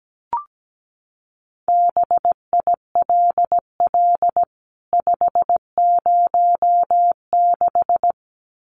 Morse code, heard as a keyed tone, about 17 wpm, 700 Hz. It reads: BILL 506
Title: CWT Exchange 17wpm